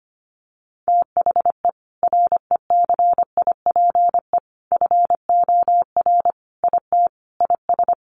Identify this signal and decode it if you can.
Morse code, heard as a keyed tone, about 25 wpm, 700 Hz. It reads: THE RECIPE FOR IT IS
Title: Sentences from Top 1200 Words 25wpm